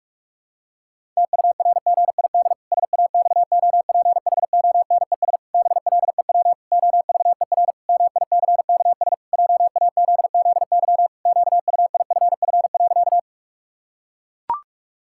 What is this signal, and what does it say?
Morse code, heard as a keyed tone, about 41 wpm, 700 Hz. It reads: TURGID SAXOPHONES BLEW OVER MICKS JAZZY QUIFF.
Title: Pangram Challenge v2 - 45wpm